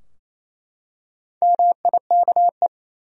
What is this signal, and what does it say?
Morse code, heard as a keyed tone, about 27 wpm, 700 Hz. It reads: MIKE
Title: USA Common Names - Max 4 Letters - 28wpm